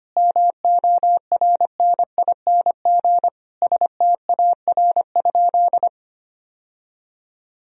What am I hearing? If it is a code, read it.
Morse code, heard as a keyed tone, about 25 wpm, 700 Hz. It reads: MORNING STAR?
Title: Sentences from Top 1000 Words - Review Part 2 - 25wpm